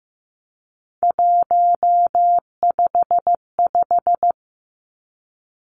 Morse code, heard as a keyed tone, about 15 wpm, 700 Hz. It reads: 155
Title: Three Challenging Letter-Number Combinations 15wpm